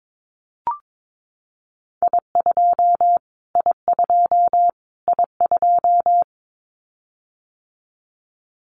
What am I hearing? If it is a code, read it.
Morse code, heard as a keyed tone, about 22 wpm, 700 Hz. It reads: I2 I2 I2
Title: Callsign Prefixes for Top 25 DXCC - Repeated 3x - 22wpm